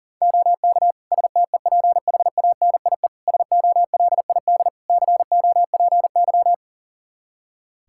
Morse code, heard as a keyed tone, about 40 wpm, 700 Hz. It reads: OK STEPHANIE SOLID COPY
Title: Practice QSO - Exchange 3 - More Info - 40wpm